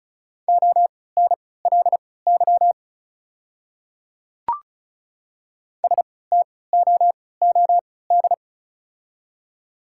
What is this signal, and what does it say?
Morse code, heard as a keyed tone, about 34 wpm, spaced as for 20 wpm, 690 Hz. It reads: ONLY STOOD
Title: 3x Character Spacing - Top 500 Words - 35wpm